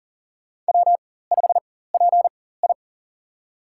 Morse code, heard as a keyed tone, about 37 wpm, spaced as for 18 wpm, 700 Hz. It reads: W5PI
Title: USA Call Signs - 4 Characters - 4x Character Spacing - 40wpm